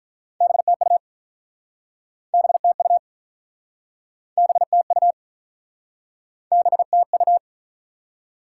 Morse code, heard as a keyed tone, about 47 wpm, 700 Hz. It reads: BTU BTU BTU BTU